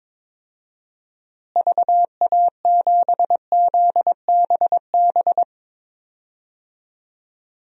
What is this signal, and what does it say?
Morse code, heard as a keyed tone, about 22 wpm, 700 Hz. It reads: VA7ZBB
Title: DXCC-100 Call Signs - 6 Characters - Repeated 2x - 22wpm